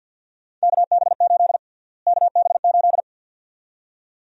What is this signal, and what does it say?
Morse code, heard as a keyed tone, about 48 wpm, 700 Hz. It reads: KB8 KB8
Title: USA Call Sign Prefixes - Repeated 2x - 50wpm